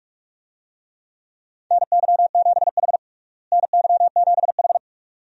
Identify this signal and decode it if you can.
Morse code, heard as a keyed tone, about 46 wpm, 700 Hz. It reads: NY7H NY7H
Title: USA Call Signs - 4 Characters - Repeated 2x - 45wpm